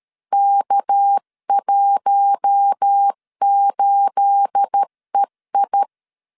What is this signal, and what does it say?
Morse code, heard as a keyed tone, about 13 wpm, 800 Hz. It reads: K18EI
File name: LEZIONI / 9 28 12wpm 2351